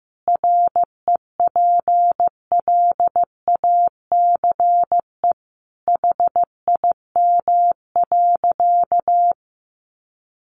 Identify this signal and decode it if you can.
Morse code, heard as a keyed tone, about 15 wpm, 700 Hz. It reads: REPLACE HIM.
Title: Sentences from Top 1400 Words 15wpm